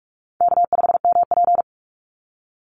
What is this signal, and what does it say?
Morse code, heard as a keyed tone, about 44 wpm, 700 Hz. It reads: K5MP
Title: Easy Call Signs - 45wpm